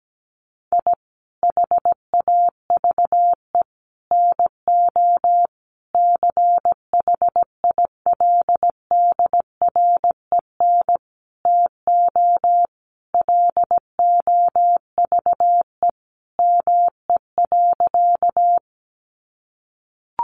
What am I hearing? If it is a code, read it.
Morse code, heard as a keyed tone, about 17 wpm, 700 Hz. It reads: I HAVE NO CHILDREN TO LOVE ME.
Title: Sentences from Top 400 Words 17wpm